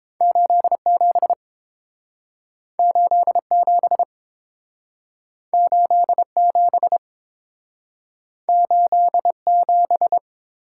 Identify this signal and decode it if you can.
Morse code, heard as a keyed tone, about 30 wpm, 700 Hz. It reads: 87 87 87 87